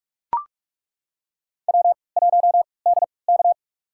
Morse code, heard as a keyed tone, about 42 wpm, spaced as for 25 wpm, 700 Hz. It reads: W1DK
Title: USA Call Signs - 3x Character Spacing - 45wpm